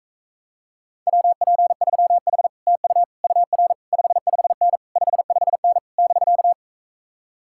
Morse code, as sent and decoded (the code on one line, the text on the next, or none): .-- .--. ...-- .... / - ..- / ..- .-. / ..... ..... -. / ..... ..... -. / -...-.-
WP3H TU UR 55N 55N <BK>